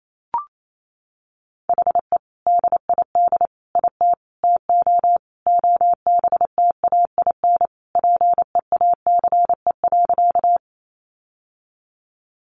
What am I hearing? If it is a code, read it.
Morse code, heard as a keyed tone, about 28 wpm, 700 Hz. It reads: HE DID IT TO OBTAIN PEACE.
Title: Sentences from Top 1500 Words 28wpm